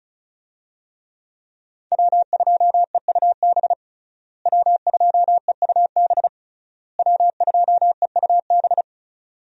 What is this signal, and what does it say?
Morse code, heard as a keyed tone, about 36 wpm, 700 Hz. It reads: W2EUB W2EUB W2EUB
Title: USA Call Signs - 5 Characters - Repeated 3x - 35wpm